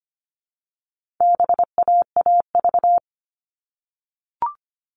Morse code, heard as a keyed tone, about 25 wpm, 700 Hz. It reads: BAAV
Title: All Cut Numbers - 4-Digit 25wpm